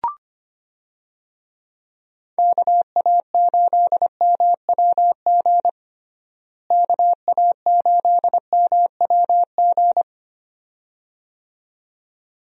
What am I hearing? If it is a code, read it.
Morse code, heard as a keyed tone, about 25 wpm, 700 Hz. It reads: KA8MWG KA8MWG
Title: USA Call Signs - 6 Characters - Repeated 2x - 25wpm